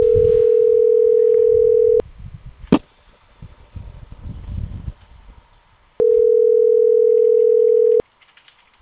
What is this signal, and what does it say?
Ringback tone (North America), 440 + 480 Hz.